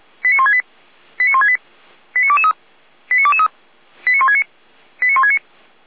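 Five-tone selective call (ZVEI), heard as quick runs of five tones, tones of about 70 ms. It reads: 89168 89168 89223 89223 89169 89169
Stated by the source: ZVEI five-tone selective call 89073, received on an SDR, from a Feuerwehr Orlamünde fire brigade server.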